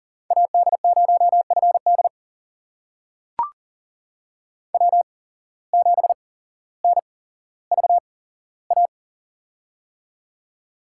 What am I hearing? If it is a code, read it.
Morse code, heard as a keyed tone, about 37 wpm, spaced as for 17 wpm, 700 Hz. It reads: AD0FD W7NVA
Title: USA Call Signs - 8x Character Spacing - 40wpm